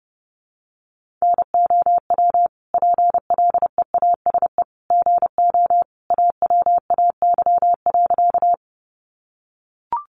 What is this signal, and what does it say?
Morse code, heard as a keyed tone, about 28 wpm, 700 Hz. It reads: NOW PLEASE GO AWAY.
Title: Sentences from Top 1500 Words 30wpm